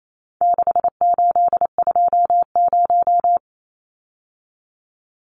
Morse code, heard as a keyed tone, about 28 wpm, 700 Hz. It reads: 6820
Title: Numbers - 4-Digit 28wpm